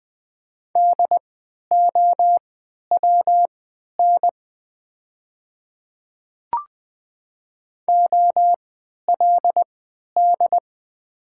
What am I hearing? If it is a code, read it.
Morse code, heard as a keyed tone, about 20 wpm, spaced as for 11 wpm, 690 Hz. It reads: DOWN OLD
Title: 3x Character Spacing - Top 300 Words - 20wpm